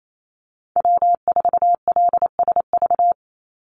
Morse code, heard as a keyed tone, about 28 wpm, 700 Hz. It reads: W4LSV
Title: Call Signs Repeated 3x - 28wpm